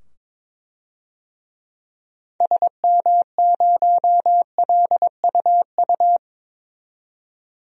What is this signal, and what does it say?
Morse code, heard as a keyed tone, about 22 wpm, 700 Hz. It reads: SM0LUU
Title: DXCC-100 Call Signs - 6 Characters - 22wpm